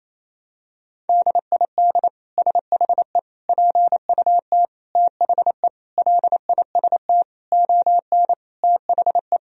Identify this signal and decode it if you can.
Morse code, heard as a keyed tone, about 28 wpm, 700 Hz. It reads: DID SHE PUT THE LIST ON THE
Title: Sentences from Top 1000 Words - Review Part 2 - 28wpm